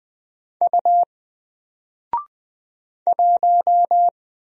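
Morse code, heard as a keyed tone, about 20 wpm, 700 Hz. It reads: U 1
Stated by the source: Single letter and number Morse drill at 20 wpm, from N4LCD.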